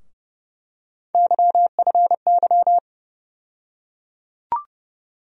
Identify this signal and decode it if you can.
Morse code, heard as a keyed tone, about 30 wpm, 700 Hz. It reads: YFY
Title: USA Call Sign Postfixes - 3 Letters 30wpm